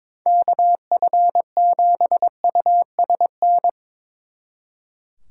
Morse code, heard as a keyed tone, about 24 wpm, 700 Hz. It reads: KF7USN
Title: Speed Racing - USA 2x3 Call Signs 22wpm